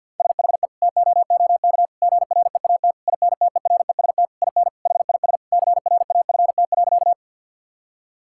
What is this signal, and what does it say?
Morse code, heard as a keyed tone, about 50 wpm, 700 Hz. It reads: SHE TOOK GREAT INTEREST IN HIS CRAFT.
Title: Sentences from Top 1000 Words - Review Part 6 - 50wpm